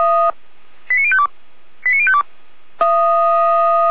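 Five-tone selective call (ZVEI), heard as quick runs of five tones, tones of about 70 ms. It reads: 89052 89052
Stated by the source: ZVEI five-tone selective call 89085 received by SDR.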